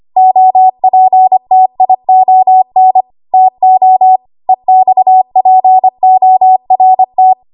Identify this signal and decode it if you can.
Morse code, heard as a keyed tone, about 25 wpm, 750 Hz. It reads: OPTION TO EXPORT
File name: Archive / 25 WPM / 25WPM